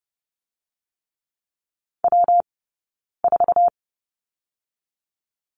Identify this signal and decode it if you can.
Morse code, heard as a keyed tone, about 30 wpm, spaced as for 9 wpm, 700 Hz. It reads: W4
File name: Lesson 20d - Number 3 - Two Characters 30wpm